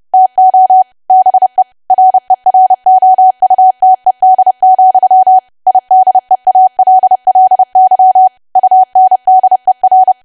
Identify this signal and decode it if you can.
Morse code, heard as a keyed tone, about 30 wpm, 750 Hz. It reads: TO BE REROUTED, IDEALLY UNDER
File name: cwa / W1AW-Level3 / 30WPM